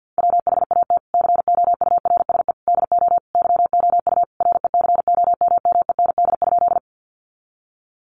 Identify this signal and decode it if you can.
Morse code, heard as a keyed tone, about 50 wpm, 700 Hz. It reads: WHAT COURSE DO YOU RECOMMEND?
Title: Sentences from Top 1200 Words 50wpm